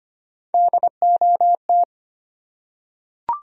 Morse code, heard as a keyed tone, about 25 wpm, 700 Hz. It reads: DOT
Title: USA Common Names - Max 5 Letters - 25wpm